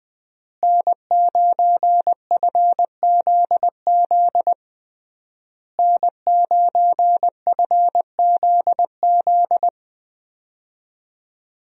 Morse code, heard as a keyed tone, about 20 wpm, 700 Hz. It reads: N9FZZ N9FZZ
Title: USA Call Signs - 5 Characters - Repeated 3x - 20wpm